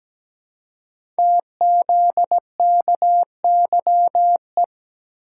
Morse code, heard as a keyed tone, about 17 wpm, 700 Hz. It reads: TZKYE
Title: Code Groups - 5 Letters - 17wpm